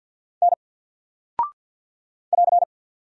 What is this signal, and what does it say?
Morse code, heard as a keyed tone, about 48 wpm, 700 Hz. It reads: N <AR>
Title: Single Character (Letter:Number:Punctuation:Prosign) Alt 50wpm